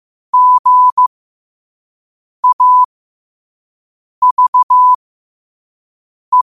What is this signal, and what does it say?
Morse code, heard as a keyed tone, about 15 wpm, spaced as for 5 wpm, 1000 Hz. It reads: GAVE